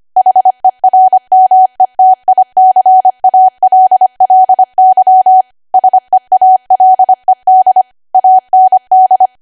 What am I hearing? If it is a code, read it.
Morse code, heard as a keyed tone, about 25 wpm, 750 Hz. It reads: HERMETICALLY SEALED AND
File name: cwa / W1AW-Level3 / 25WPM